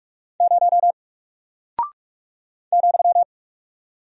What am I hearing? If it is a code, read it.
Morse code, heard as a keyed tone, about 46 wpm, 700 Hz. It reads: 0 ,
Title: Single Character (Letter:Number:Punctuation:Prosign) Alt 45wpm